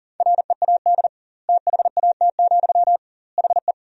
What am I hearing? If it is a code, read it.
Morse code, heard as a keyed tone, about 40 wpm, 700 Hz. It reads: READ THAT, HE
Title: 2x Word-Spacing - Sentences from Top 200 Words - 40wpm